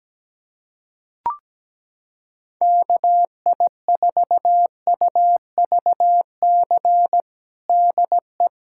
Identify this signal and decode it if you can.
Morse code, heard as a keyed tone, about 17 wpm, 700 Hz. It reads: KI4UVC DE
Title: USA Callsign DE Callsign - 6 Characters - 17wpm